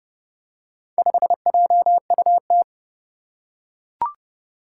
Morse code, heard as a keyed tone, about 29 wpm, 700 Hz. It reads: 5JUT
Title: Four Challenging Letter-Number Combinations 30wpm